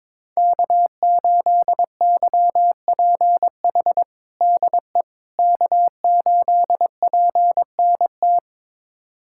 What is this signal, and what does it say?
Morse code, heard as a keyed tone, about 22 wpm, 700 Hz. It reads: K8YPH DE K8PNT
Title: USA Callsign DE Callsign - 5 Characters - 22wpm